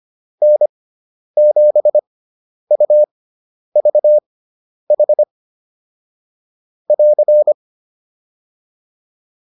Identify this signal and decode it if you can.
Morse code, heard as a keyed tone, about 25 wpm, spaced as for 10 wpm, 600 Hz. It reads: N7UVH <AR>